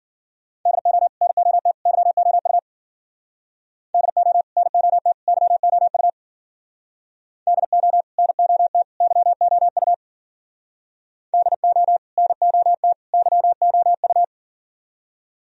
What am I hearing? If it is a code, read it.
Morse code, heard as a keyed tone, about 49 wpm, 700 Hz. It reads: DO NOT YOU DO NOT YOU DO NOT YOU DO NOT YOU
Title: Speed-Racing - Sets of 3 Words 40wpm